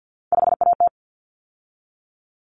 Morse code, heard as a keyed tone, about 50 wpm, 700 Hz. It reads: 5AT
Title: Common Cut Numbers - 3-Digit 50wpm